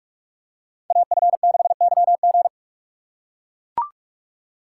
Morse code, heard as a keyed tone, about 42 wpm, 700 Hz. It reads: AF6YG
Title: USA Call Signs - 4x Character Spacing - 45wpm